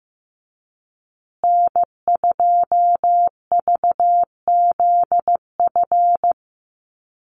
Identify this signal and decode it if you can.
Morse code, heard as a keyed tone, about 15 wpm, 700 Hz. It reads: N2VZF
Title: USA Call Signs Repeated 3x 15wpm